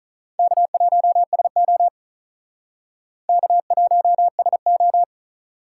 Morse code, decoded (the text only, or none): K1SO K1SO